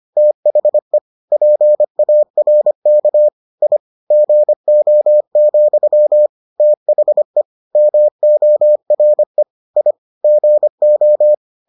Morse code, heard as a keyed tone, about 25 wpm, 600 Hz. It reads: THE PARK I GO, THE MORE I GO